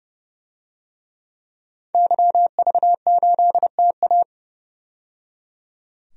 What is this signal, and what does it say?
Morse code, heard as a keyed tone, about 30 wpm, 700 Hz. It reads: YV8TA
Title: DXCC-100 Call Signs - 5 Characters - 30wpm